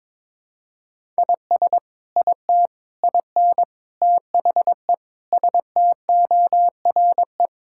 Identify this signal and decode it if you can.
Morse code, heard as a keyed tone, about 22 wpm, 700 Hz. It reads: IS IT IN THE STORE
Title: Sentences from Top 1000 Words - Review Part 5 - 22wpm